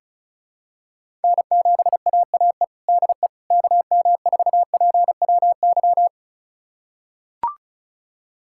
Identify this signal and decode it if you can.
Morse code, heard as a keyed tone, about 34 wpm, 700 Hz. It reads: N7AAE DE KM4PWY
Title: USA Callsign DE Callsign 35wpm